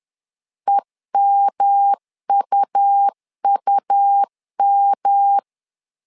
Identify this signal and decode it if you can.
Morse code, heard as a keyed tone, about 11 wpm, 800 Hz. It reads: EMUUM
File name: LEZIONI / 3 MWRO / 3 M 10wpm 1828